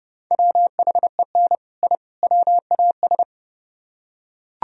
Morse code, heard as a keyed tone, about 30 wpm, 700 Hz. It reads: WHEN I WAS